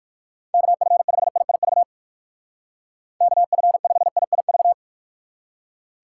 Morse code, heard as a keyed tone, about 59 wpm, 700 Hz. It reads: KF5IIV KF5IIV